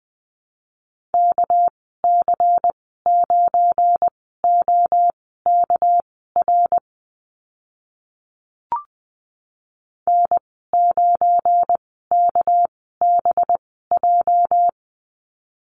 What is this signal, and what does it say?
Morse code, heard as a keyed tone, about 20 wpm, spaced as for 14 wpm, 700 Hz. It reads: KC9OKR N9KBJ
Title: 231 Lesson 37g - Letter x - Call Signs 20wpm